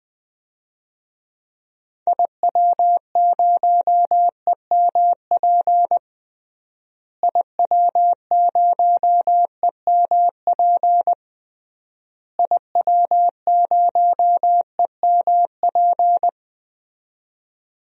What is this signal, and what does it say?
Morse code, heard as a keyed tone, about 20 wpm, 700 Hz. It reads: IW0EMP IW0EMP IW0EMP
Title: DXCC-100 Call Signs - 6 Characters - Repeated 3x - 20wpm